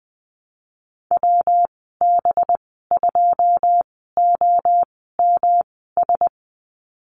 Morse code, heard as a keyed tone, about 20 wpm, spaced as for 14 wpm, 700 Hz. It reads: WB2OMS
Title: 193 Lesson 31g - Letter v - Call Signs 20wpm